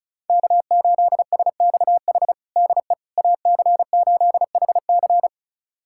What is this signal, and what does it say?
Morse code, heard as a keyed tone, about 36 wpm, 700 Hz. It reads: K8SXH DE AC8HC
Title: USA Callsign DE Callsign 35wpm